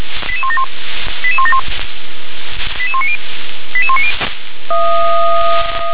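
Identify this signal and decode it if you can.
Five-tone selective call (ZVEI), heard as quick runs of five tones, tones of about 70 ms. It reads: 88171 88171 88190 88190
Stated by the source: ZVEI five-tone selective call received on SDR.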